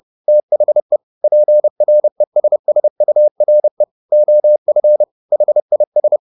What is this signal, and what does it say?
Morse code, heard as a keyed tone, about 30 wpm, 600 Hz. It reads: THE PRESSURE OF HIS
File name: the old man and the sea / the old man and the sea 1 30wpm